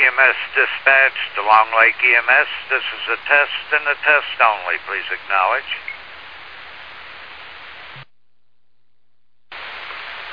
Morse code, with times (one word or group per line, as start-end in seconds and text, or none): none